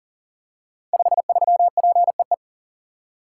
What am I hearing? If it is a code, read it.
Morse code, heard as a keyed tone, about 40 wpm, 700 Hz. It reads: 53PEE
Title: Code Groups - 5 Letter:Numbers - 40wpm